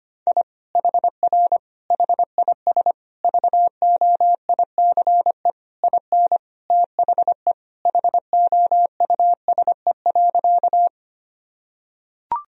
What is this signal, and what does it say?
Morse code, heard as a keyed tone, about 25 wpm, 700 Hz. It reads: I HR HIS VOICE IN THE HOUSE.
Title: Sentences from Top 500 Words and Abbreviations 25wpm